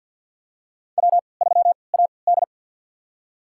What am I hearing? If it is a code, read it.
Morse code, heard as a keyed tone, about 50 wpm, spaced as for 28 wpm, 700 Hz. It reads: W3AD
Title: USA Call Signs - 4 Characters - 3x Character Spacing - 50wpm